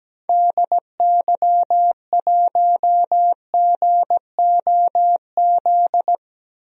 Morse code, heard as a keyed tone, about 17 wpm, 700 Hz. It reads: DY1GOZ
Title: DXCC-100 Call Signs - 6 Characters - Repeated 2x - 17wpm